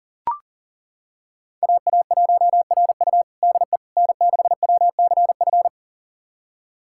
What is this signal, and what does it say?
Morse code, heard as a keyed tone, about 40 wpm, 700 Hz. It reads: AA1RU DE N6WCF
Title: USA Callsign DE Callsign - 5 Characters - 40wpm